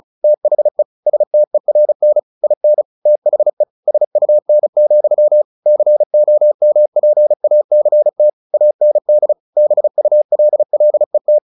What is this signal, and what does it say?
Morse code, heard as a keyed tone, about 36 wpm, 600 Hz. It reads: THE STERN IN THE SUN, COMPACT AND BULLET
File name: the old man and the sea / the old man and the sea 1 35wpm